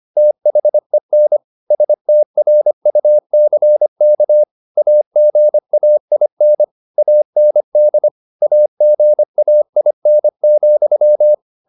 Morse code, heard as a keyed tone, about 25 wpm, 600 Hz. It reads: THEN STRUCK AGAIN AND AGAIN,
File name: the old man and the sea part 1 25wpm